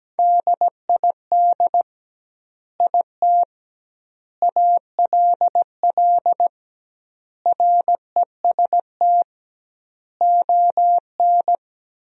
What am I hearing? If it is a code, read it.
Morse code, heard as a keyed tone, about 17 wpm, 700 Hz. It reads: DID IT ALL REST ON
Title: 2x Word-Spacing - Sentences from Top 500 Words - 17wpm